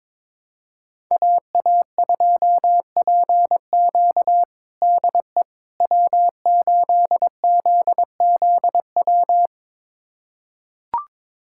Morse code, heard as a keyed tone, about 22 wpm, 700 Hz. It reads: AA2PQ DE W8ZZW
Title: USA Callsign DE Callsign 22wpm